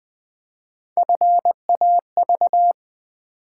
Morse code, heard as a keyed tone, about 20 wpm, 700 Hz. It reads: FAV
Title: Build A Word - 3 Letters At A Time - 9-Letter Words - 20wpm